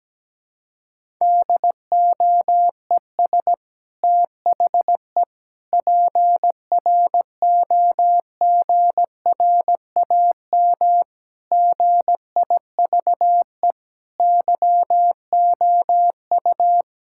Morse code, heard as a keyed tone, about 17 wpm, 700 Hz. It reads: DOES THE PROGRAM GIVE YOU
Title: Sentences from Top 1000 Words - Review Part 5 - 17wpm